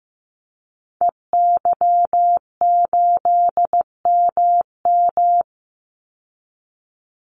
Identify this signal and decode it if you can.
Morse code, heard as a keyed tone, about 15 wpm, 700 Hz. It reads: EY8MM
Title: Call Signs Repeated 3x - 15wpm